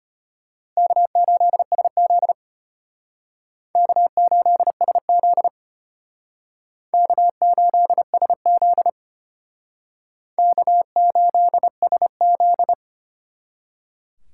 Morse code, heard as a keyed tone, about 32 wpm, 700 Hz. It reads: K8SZ K8SZ K8SZ K8SZ